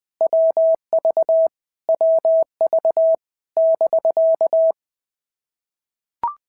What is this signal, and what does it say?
Morse code, heard as a keyed tone, about 20 wpm, 650 Hz. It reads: WV WV <BK>
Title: POTA QSO Practice - 20wpm